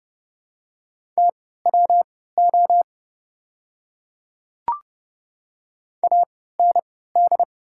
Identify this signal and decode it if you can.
Morse code, heard as a keyed tone, about 32 wpm, spaced as for 17 wpm, 700 Hz. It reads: TWO AND
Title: 3x Character Spacing - Top 100 Words - 30wpm